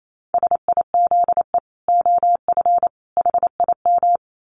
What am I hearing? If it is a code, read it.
Morse code, heard as a keyed tone, about 28 wpm, 700 Hz. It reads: SIZE OF HIM
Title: Dit Challenge 28wpm